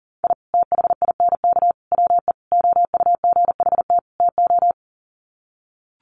Morse code, heard as a keyed tone, about 40 wpm, 700 Hz. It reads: I THINK WE OUGHT TO
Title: Sets of 5 Words Encore 40wpm